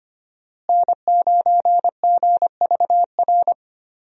Morse code, heard as a keyed tone, about 25 wpm, 700 Hz. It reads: N9GVR
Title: USA Call Signs - 5 Characters - Repeated 2x - 25wpm